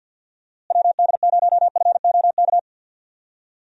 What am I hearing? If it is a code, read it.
Morse code, heard as a keyed tone, about 51 wpm, 700 Hz. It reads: WD0FOK
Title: USA Call Signs - 6 Characters - Repeated 3x - 50wpm